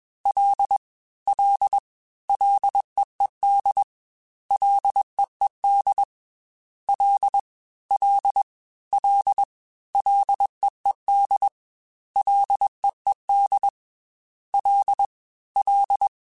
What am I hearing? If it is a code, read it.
Morse code, heard as a keyed tone, about 21 wpm, 800 Hz. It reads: L L LEED LEED L L L LEED LEED L L
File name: F and L / F and L mp3 / F and L Prefixes 3x2 Twice @20